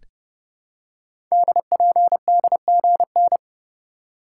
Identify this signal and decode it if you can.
Morse code, heard as a keyed tone, about 30 wpm, 700 Hz. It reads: DPDGN
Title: Code Groups - 5 Letters - 30wpm